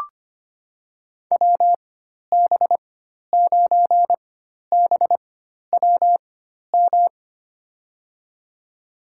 Morse code, heard as a keyed tone, about 25 wpm, spaced as for 12 wpm, 700 Hz. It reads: WB9BWM
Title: USA Call Signs - 4x Character Spacing - 25wpm